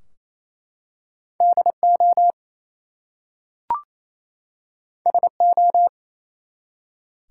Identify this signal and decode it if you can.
Morse code, heard as a keyed tone, about 28 wpm, 700 Hz. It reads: DO SO
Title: Two Letter Words - 28wpm